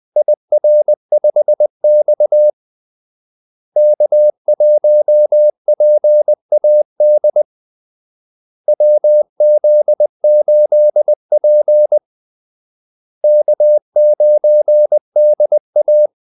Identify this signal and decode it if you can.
Morse code, heard as a keyed tone, about 20 wpm, 600 Hz. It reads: IR5X K1PAD WZ8P K9DA
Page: Kaldesignaler 20 wpm